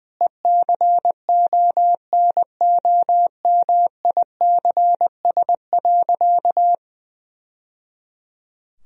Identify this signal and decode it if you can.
Morse code, heard as a keyed tone, about 21 wpm, 700 Hz. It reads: ECONOMICS.
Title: Sentences from Top 1000 Words - Review Part 3 - 20wpm